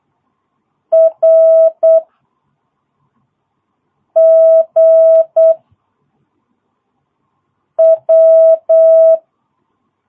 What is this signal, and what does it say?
Morse code, heard as a keyed tone, about 8 wpm, 660 Hz. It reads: R G W
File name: jh8cbh / 2022cw